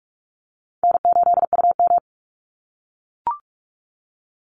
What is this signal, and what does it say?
Morse code, heard as a keyed tone, about 44 wpm, 700 Hz. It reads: N8UM
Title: Call Signs - 45wpm